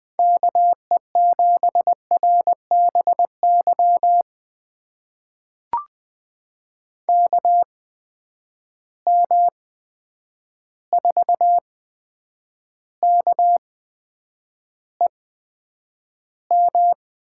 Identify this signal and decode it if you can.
Morse code, heard as a keyed tone, about 21 wpm, spaced as for 9 wpm, 700 Hz. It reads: KE7RBY KM4KEM
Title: USA Call Signs - 6 Characters - 8x Character Spacing - 20wpm